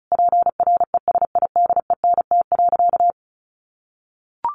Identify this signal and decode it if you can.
Morse code, heard as a keyed tone, about 34 wpm, 700 Hz. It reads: PRESIDENT.
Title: Sentences from Top 1500 Words 35wpm